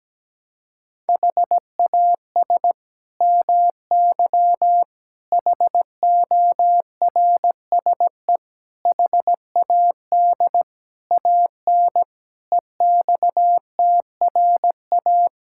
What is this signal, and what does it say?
Morse code, heard as a keyed tone, about 17 wpm, 700 Hz. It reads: HAS MY HORSE HAD AN EXTRA
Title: Sentences from Top 1600 Words 17wpm